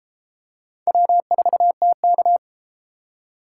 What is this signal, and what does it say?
Morse code, heard as a keyed tone, about 32 wpm, 700 Hz. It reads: W4TK